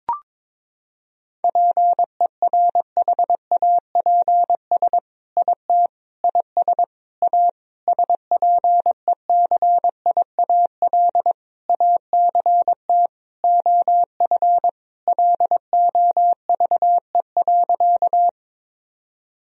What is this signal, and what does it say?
Morse code, heard as a keyed tone, about 22 wpm, 700 Hz. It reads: PERHAPS IT IS A SPECIAL ACT OF LOVE.
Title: Sentences from Top 1000 Words - Review Part 3 - 22wpm